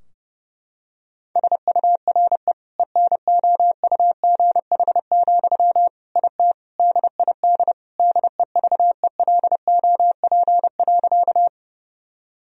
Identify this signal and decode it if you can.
Morse code, heard as a keyed tone, about 30 wpm, 700 Hz. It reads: SURE ENOUGH, IT DID DEVELOP.
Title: Sentences from Top 1000 Words - Review Part 2 - 30wpm